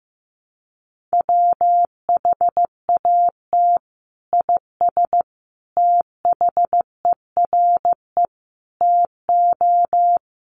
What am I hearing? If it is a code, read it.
Morse code, heard as a keyed tone, about 15 wpm, 700 Hz. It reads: WHAT IS THERE TO
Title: Sentences from Top 100 Words 15wpm